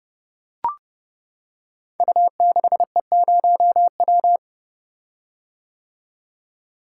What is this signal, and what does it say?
Morse code, heard as a keyed tone, about 30 wpm, 700 Hz. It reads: U6E0W
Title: Code Groups - 5 Letter:Numbers - 30wpm